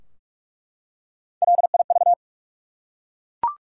Morse code, heard as a keyed tone, about 47 wpm, 700 Hz. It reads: LIV
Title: USA Common Names - Max 4 Letters - 45wpm